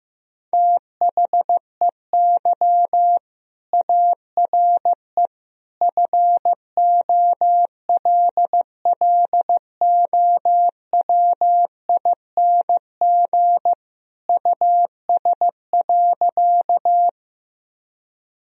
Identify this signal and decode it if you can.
Morse code, heard as a keyed tone, about 16 wpm, 700 Hz. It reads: THEY ARE FOLLOWING US.
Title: Sentences from Top 1000 Words - Review Part 4 - 15wpm